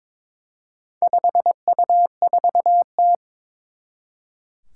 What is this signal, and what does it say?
Morse code, heard as a keyed tone, about 22 wpm, 700 Hz. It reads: 5U4T
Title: Four Challenging Letter-Number Combinations 22wpm